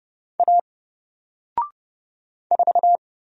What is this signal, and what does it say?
Morse code, heard as a keyed tone, about 31 wpm, 700 Hz. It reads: A 4
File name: Single-Letter-Number-30wpm